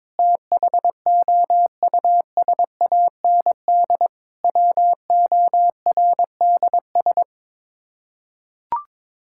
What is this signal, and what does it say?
Morse code, heard as a keyed tone, about 21 wpm, 700 Hz. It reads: THOUSAND WORDS
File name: practice / 22wpm / Sets of 7 Words - 22wpm